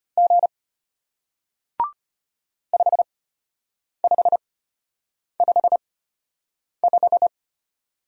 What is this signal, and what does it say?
Morse code, heard as a keyed tone, about 34 wpm, spaced as for 8 wpm, 700 Hz. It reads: G 5555